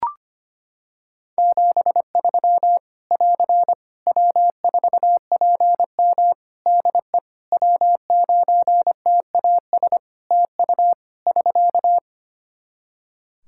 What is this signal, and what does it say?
Morse code, heard as a keyed tone, about 25 wpm, 700 Hz. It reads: 73 <AR> W4PM DE W9TAS TU <SK>
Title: K1USN SST QSO Practice 25wpm